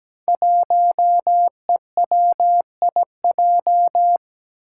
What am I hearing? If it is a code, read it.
Morse code, heard as a keyed tone, about 17 wpm, 700 Hz. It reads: 1EWIJ
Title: Five Challenging Letter-Number Combinations 17wpm